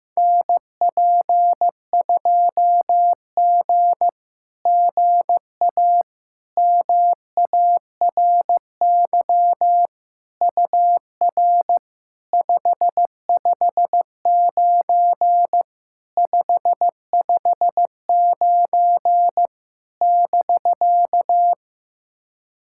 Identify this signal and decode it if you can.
Morse code, heard as a keyed tone, about 15 wpm, 700 Hz. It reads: NP2G GA MARY UR 559 559 <BK>
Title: SOTA QSO Practice 15wpm